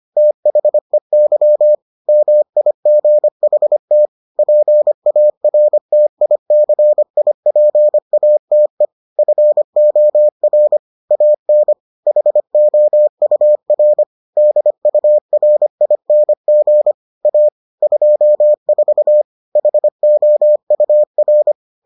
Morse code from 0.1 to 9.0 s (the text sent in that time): THEY MIGHT PARTICIPATE